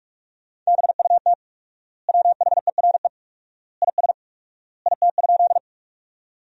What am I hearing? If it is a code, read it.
Morse code, heard as a keyed tone, about 41 wpm, 700 Hz. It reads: BUT WHERE IS IT?
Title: 4x Word-Spacing - Sentences from Top 200 Words - 45wpm